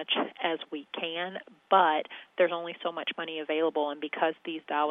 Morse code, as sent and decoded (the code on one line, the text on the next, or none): none